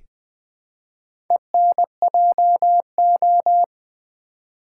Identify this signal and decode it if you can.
Morse code, heard as a keyed tone, about 20 wpm, 700 Hz. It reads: ENJO
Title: Build A Word - 1 Letter At A Time - 5-Letter Words - 20wpm